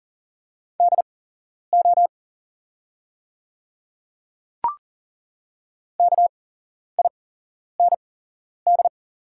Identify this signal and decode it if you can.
Morse code, heard as a keyed tone, about 40 wpm, spaced as for 11 wpm, 690 Hz. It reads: DO KIND
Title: 8x Character Spacing - Top 300 Words - 40wpm